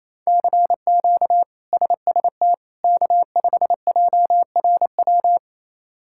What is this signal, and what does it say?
Morse code, heard as a keyed tone, about 28 wpm, 700 Hz. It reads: CQ SST K5JRW
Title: K1USN SST QSO Practice - 28wpm